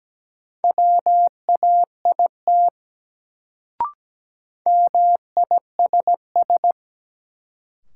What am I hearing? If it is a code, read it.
Morse code, heard as a keyed tone, about 17 wpm, 700 Hz. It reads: WAIT MISS
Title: Sentences from Top 500 Words - New Word Drill - 17wpm